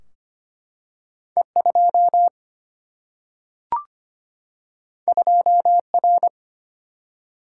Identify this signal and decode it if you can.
Morse code, heard as a keyed tone, about 25 wpm, 700 Hz. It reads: E 2 2 R
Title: Single-Letter Single-Number 25wpm